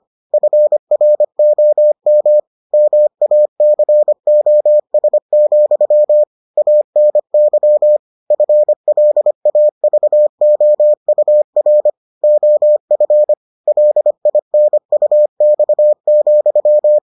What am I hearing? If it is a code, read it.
Morse code, heard as a keyed tone, about 25 wpm, 600 Hz. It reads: FROM MACOS, ANY FLAVOUR OF LINUX,